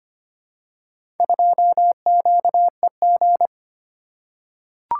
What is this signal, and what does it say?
Morse code, heard as a keyed tone, about 25 wpm, 700 Hz. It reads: 2QEG
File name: practice / 25wpm / Four Letter-Number Combinations - 25wpm